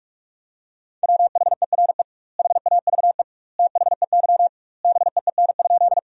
Morse code, heard as a keyed tone, about 45 wpm, 700 Hz. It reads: WHERE HAVE THEY BEEN?
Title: 2x Word-Spacing - Sentences from Top 200 Words - 45wpm